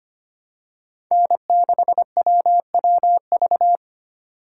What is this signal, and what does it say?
Morse code, heard as a keyed tone, about 25 wpm, 700 Hz. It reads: N6WWV
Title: USA Call Signs - 5 Characters - 4x Character Spacing - 25wpm